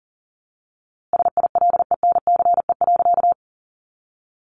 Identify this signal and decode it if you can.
Morse code, heard as a keyed tone, about 40 wpm, 700 Hz. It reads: SILENCE.
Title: A Sound Of Thunder 40wpm